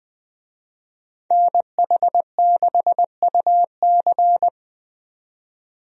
Morse code, heard as a keyed tone, about 21 wpm, 700 Hz. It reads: NH6UC
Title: USA Call Signs - 5 Characters - 20wpm